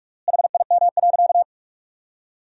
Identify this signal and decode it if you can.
Morse code, heard as a keyed tone, about 46 wpm, 700 Hz. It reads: HIM.
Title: 3x Word-Spacing - Sentences from Top 400 Words - 45wpm